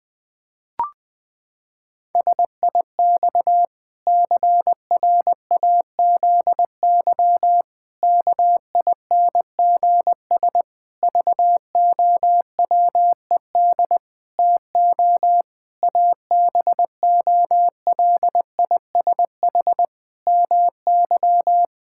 Morse code, heard as a keyed tone, about 21 wpm, 700 Hz. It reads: SIX CRAZY KINGS VOWED TO ABOLISH MY
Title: Pangram Challenge v2 - 20wpm